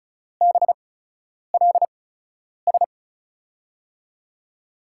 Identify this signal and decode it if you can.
Morse code, heard as a keyed tone, about 36 wpm, spaced as for 10 wpm, 700 Hz. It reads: BLS
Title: USA Call Signs - 8x Character Spacing - 35wpm